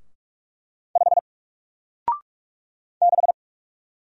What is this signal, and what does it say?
Morse code, heard as a keyed tone, about 46 wpm, 700 Hz. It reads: 5 6
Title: Single Letter-Number Alt 45wpm